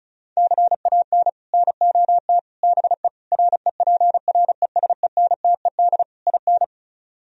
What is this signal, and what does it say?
Morse code, heard as a keyed tone, about 45 wpm, 700 Hz. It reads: CAN NOT BE REPRESENTED IN